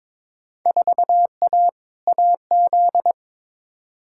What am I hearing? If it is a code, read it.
Morse code, heard as a keyed tone, about 22 wpm, 700 Hz. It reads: 4A AZ
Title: Field Day Exchanges Encore - 22wpm